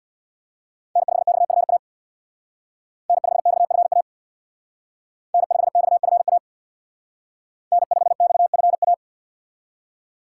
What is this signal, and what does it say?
Morse code, heard as a keyed tone, about 63 wpm, 700 Hz. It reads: N5XFA N5XFA N5XFA N5XFA